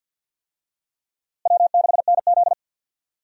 Morse code, heard as a keyed tone, about 52 wpm, 700 Hz. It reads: W6NZ
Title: USA Call Signs - 4 Characters - Repeated 2x - 50wpm